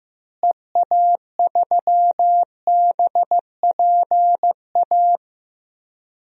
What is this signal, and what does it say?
Morse code, heard as a keyed tone, about 16 wpm, 700 Hz. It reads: EA3BPA
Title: DXCC-100 Call Signs - 6 Characters - Repeated 2x - 15wpm